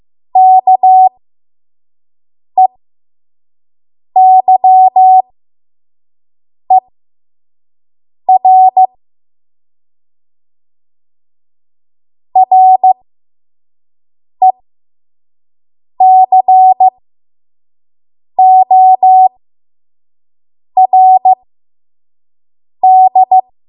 Morse code, heard as a keyed tone, about 15 wpm, spaced as for 5 wpm, 750 Hz. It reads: KEYER RECORD
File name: Morse / 05WPM